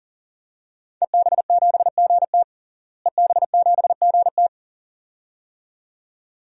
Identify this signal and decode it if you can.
Morse code, heard as a keyed tone, about 40 wpm, 700 Hz. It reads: EB7GT EB7GT
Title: DXCC-100 Call Signs - 5 Characters - Repeated 2x - 40wpm